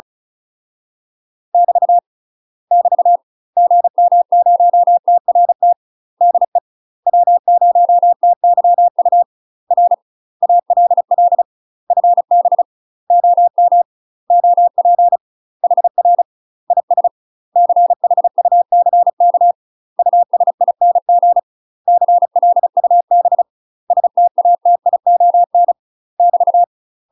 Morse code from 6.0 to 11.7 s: DE W0TYU R ALL